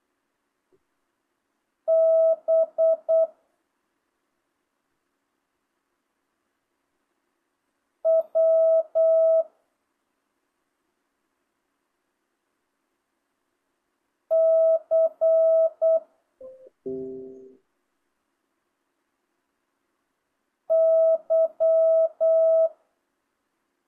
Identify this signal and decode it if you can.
Morse code, heard as a keyed tone, about 8 wpm, 650 Hz. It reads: B W C Y